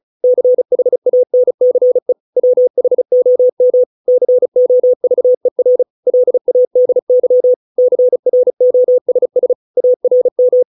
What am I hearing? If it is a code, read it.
Morse code, heard as a keyed tone, about 34 wpm, 500 Hz. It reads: CHANCE WHOM COVER LADY CROSS ARM